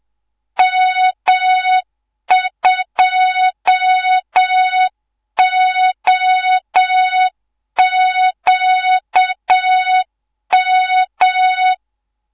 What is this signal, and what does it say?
Morse code, heard as a keyed tone, about 7 wpm, 3700 Hz, with noise. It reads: M2OQM
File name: Morze-HA2MN / hangp / 2 03 GYAKORLAT